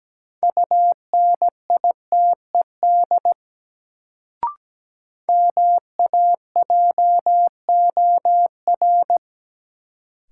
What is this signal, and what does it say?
Morse code, heard as a keyed tone, about 17 wpm, 700 Hz. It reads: UNITED MAJOR
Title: Sentences from Top 600 Words - New Word Drill - 17wpm